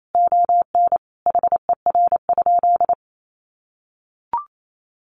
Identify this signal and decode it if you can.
Morse code, heard as a keyed tone, about 28 wpm, 700 Hz. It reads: ON HER?
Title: Sentences from Top 1200 Words 28wpm